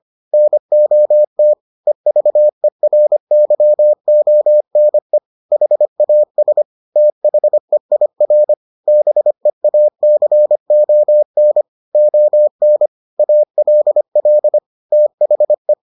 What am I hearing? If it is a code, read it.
Morse code, heard as a keyed tone, about 25 wpm, 600 Hz. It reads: NOT EVERYONE HAS THEIR BEACON ON ALL THE